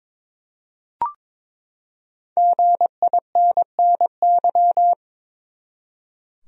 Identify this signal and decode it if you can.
Morse code, heard as a keyed tone, about 22 wpm, 700 Hz. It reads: GINNY
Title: USA Common Names - 22wpm